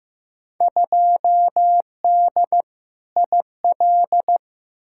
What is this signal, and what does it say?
Morse code, heard as a keyed tone, about 15 wpm, 700 Hz. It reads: 2D IL
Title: Field Day Exchanges - 15wpm